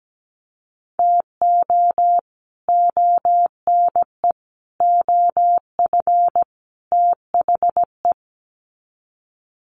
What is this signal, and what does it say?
Morse code, heard as a keyed tone, about 17 wpm, 700 Hz. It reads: TO ONE OF THE
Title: Sets of 4 Words 17wpm